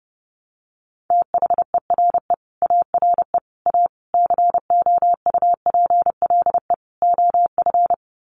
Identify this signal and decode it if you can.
Morse code, heard as a keyed tone, about 30 wpm, 700 Hz. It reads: THERE ARE A COUPLE OF
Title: Sets of 5 Words Encore 30wpm